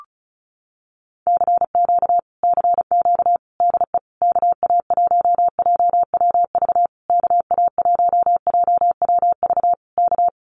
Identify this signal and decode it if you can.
Morse code, heard as a keyed tone, about 34 wpm, 700 Hz. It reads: CQ CQ DE KA1JWV KA1JWV K
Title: Common QSO Phrases Encore 35wpm